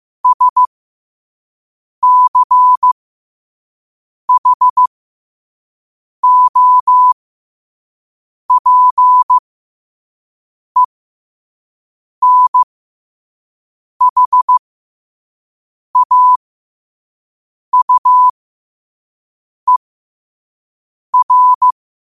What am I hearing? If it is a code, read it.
Morse code, heard as a keyed tone, about 15 wpm, spaced as for 5 wpm, 1000 Hz. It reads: SCHOPENHAUER
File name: qotdfiles / 5wpm